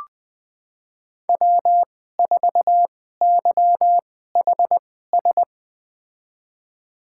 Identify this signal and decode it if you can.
Morse code, heard as a keyed tone, about 21 wpm, spaced as for 15 wpm, 700 Hz. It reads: W4YHS
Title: USA Call Signs - 2x Character Spacing - 20wpm